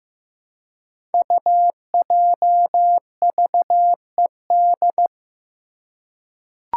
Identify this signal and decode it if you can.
Morse code, heard as a keyed tone, about 15 wpm, 700 Hz. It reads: UJVED
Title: Five Challenging Letter-Number Combinations 15wpm